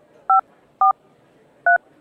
DTMF, heard as telephone touch tones, tones of about 130 ms. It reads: 543